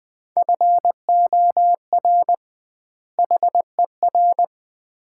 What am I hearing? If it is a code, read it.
Morse code, heard as a keyed tone, about 20 wpm, 700 Hz. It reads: FOR HER
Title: 2x Word-Spacing - Sets of 2 Words - 20wpm